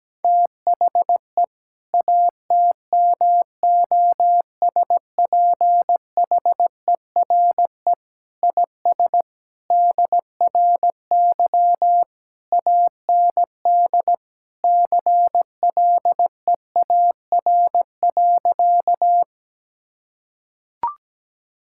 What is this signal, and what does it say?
Morse code, heard as a keyed tone, about 17 wpm, 700 Hz. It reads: THE ATMOSPHERE IS DRY AND CLEAR.
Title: Sentences from Top 1600 Words 17wpm